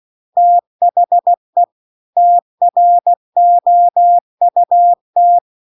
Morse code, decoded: THE TROUT

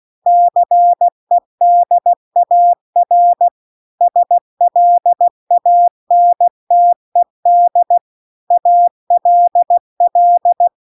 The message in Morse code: CEDAR SLANTED ALL

